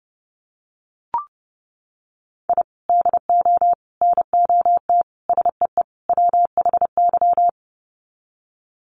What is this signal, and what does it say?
Morse code, heard as a keyed tone, about 30 wpm, 700 Hz. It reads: I DO NOT SEE WHY
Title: Sets of 5 Words Encore 30wpm